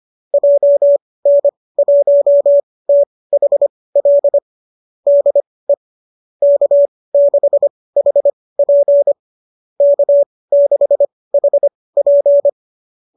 Morse code, heard as a keyed tone, about 25 wpm, spaced as for 18 wpm, 590 Hz. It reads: JN1THL DE K6HP K6HP